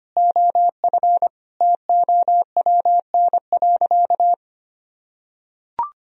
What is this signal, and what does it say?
Morse code, heard as a keyed tone, about 25 wpm, 700 Hz. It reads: OF TOWN.
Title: Sentences from Top 1000 Words - Review Part 4 - 25wpm